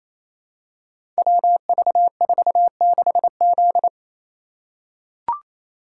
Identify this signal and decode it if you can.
Morse code, heard as a keyed tone, about 27 wpm, 700 Hz. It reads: WV46Z
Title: Code Groups - 5 Letter:Numbers - 28wpm